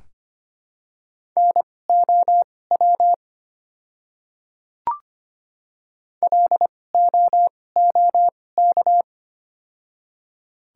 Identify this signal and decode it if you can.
Morse code, heard as a keyed tone, about 25 wpm, spaced as for 18 wpm, 700 Hz. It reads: NOW LOOK
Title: 2x Character Spacing - Top 100 Words - 25wpm